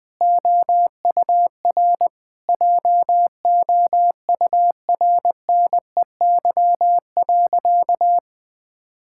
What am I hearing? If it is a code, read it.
Morse code, heard as a keyed tone, about 20 wpm, 700 Hz. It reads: OUR JOURNEY.
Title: Sentences with 9-Word Phrases - 20wpm